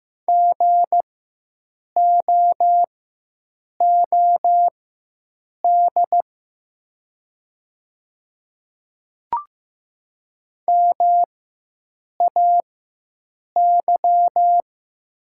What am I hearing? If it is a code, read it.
Morse code, heard as a keyed tone, about 15 wpm, spaced as for 7 wpm, 700 Hz. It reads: GOOD MAY
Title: 4x Character Spacing - Top 200 Words - 15wpm